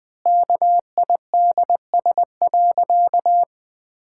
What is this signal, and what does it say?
Morse code, heard as a keyed tone, about 21 wpm, 700 Hz. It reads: KIDS.
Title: Pangram Challenge v2 - 20wpm